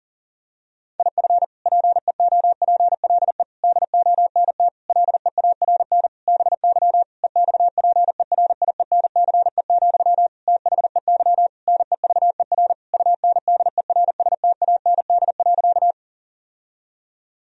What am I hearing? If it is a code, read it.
Morse code, heard as a keyed tone, about 40 wpm, 700 Hz. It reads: IF PEOPLE DONT LEARN BY EXPERIENCE, THEY NEVER UNDERSTAND.
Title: Sentences from Top 1000 Words - Review Part 3 - 40wpm